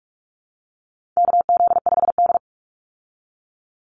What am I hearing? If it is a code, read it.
Morse code, heard as a keyed tone, about 44 wpm, 700 Hz. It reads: KZ5D
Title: Call Signs Repeated 2x - 45wpm